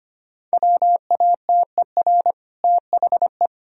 Morse code, heard as a keyed tone, about 25 wpm, 700 Hz. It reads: WATER THE
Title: Sentences from Top 1000 Words - Review Part 1 - 25wpm